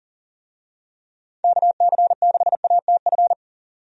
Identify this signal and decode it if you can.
Morse code, heard as a keyed tone, about 40 wpm, 700 Hz. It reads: KC6ATF
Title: USA Call Signs - 6 Characters - Repeated 3x - 40wpm